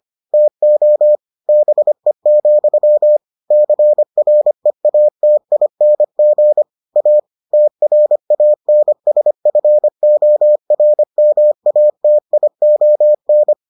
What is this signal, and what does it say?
Morse code, heard as a keyed tone, about 25 wpm, 600 Hz. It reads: TO BE, CREATING A TRANSFORMATION